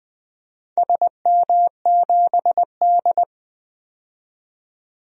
Morse code, heard as a keyed tone, about 20 wpm, 700 Hz. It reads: SM7D
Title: DXCC-100 Call Signs - 4 Characters - 20wpm